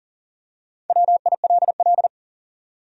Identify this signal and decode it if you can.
Morse code, heard as a keyed tone, about 40 wpm, 700 Hz. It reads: WILL
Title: USA Common Names - 40wpm